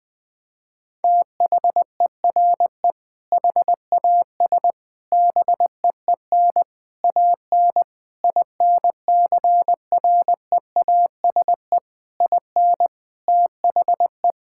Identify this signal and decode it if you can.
Morse code, heard as a keyed tone, about 21 wpm, 700 Hz. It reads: THERE HAS BEEN AN INCREASE IN THE
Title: Sentences with 6-Word Phrases - 20wpm